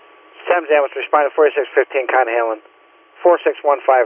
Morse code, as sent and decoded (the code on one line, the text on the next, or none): none